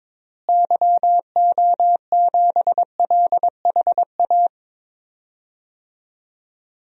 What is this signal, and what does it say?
Morse code, heard as a keyed tone, about 22 wpm, 700 Hz. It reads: YO7LHA
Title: DXCC-100 Call Signs - 6 Characters - Repeated 3x - 22wpm